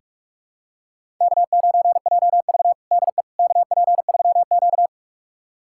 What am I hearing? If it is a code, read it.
Morse code, heard as a keyed tone, about 42 wpm, 700 Hz. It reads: K9JV DE KP3Q
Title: USA Callsign DE Callsign - 4 Characters - 45wpm